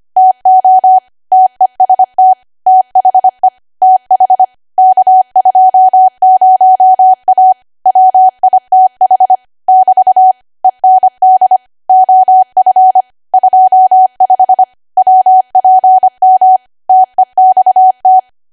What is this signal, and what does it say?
Morse code, heard as a keyed tone, about 25 wpm, 750 Hz. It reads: TO TEST THE TH K20A WITH <BT> END OF 25 WPM TEXT
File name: Archive / 25 WPM / 25WPM